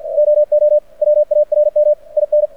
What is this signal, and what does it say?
Morse code, heard as a keyed tone, about 51 wpm, 600 Hz. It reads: OO WARM IN
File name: archived / CRSnet